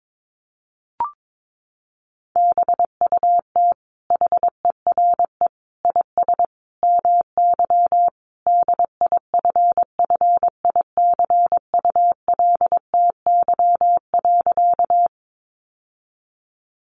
Morse code, heard as a keyed tone, about 22 wpm, 700 Hz. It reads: BUT HERE IS MY DIFFICULTY.
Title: Sentences from Top 1500 Words 22wpm